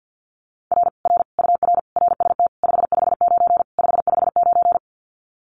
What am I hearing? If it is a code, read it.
Morse code, heard as a keyed tone, about 50 wpm, 700 Hz. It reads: R R UR RST 559 559
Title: Common QSO Phrases Encore 50wpm